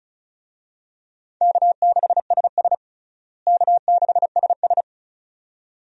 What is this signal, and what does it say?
Morse code, heard as a keyed tone, about 36 wpm, 700 Hz. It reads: K6SS K6SS
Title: USA Call Signs - 4 Characters - Repeated 2x - 35wpm